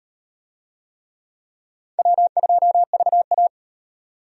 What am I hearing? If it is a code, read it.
Morse code, heard as a keyed tone, about 38 wpm, 700 Hz. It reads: W2VA